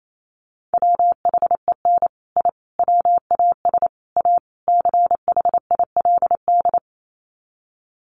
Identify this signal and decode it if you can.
Morse code, heard as a keyed tone, about 28 wpm, 690 Hz. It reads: WHEN I WAS A CHILD
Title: Sets of 5 Words Encore 28wpm